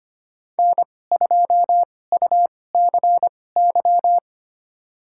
Morse code, heard as a keyed tone, about 25 wpm, spaced as for 18 wpm, 700 Hz. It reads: N2UCY
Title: USA Call Signs - 2x Character Spacing - 25wpm